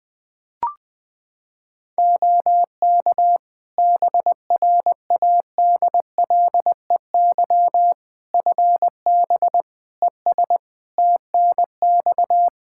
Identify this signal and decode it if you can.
Morse code, heard as a keyed tone, about 20 wpm, 700 Hz. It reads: OK BRADLEY FB ES TNX
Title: Practice QSO - Exchange 2 - Info - 20wpm